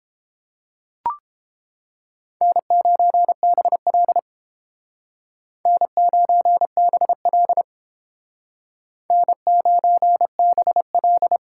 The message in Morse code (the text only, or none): N9BL N9BL N9BL